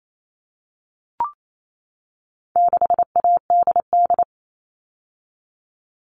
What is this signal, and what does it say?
Morse code, heard as a keyed tone, about 28 wpm, 700 Hz. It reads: 6ADD
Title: All Cut Numbers - 4-Digit 28wpm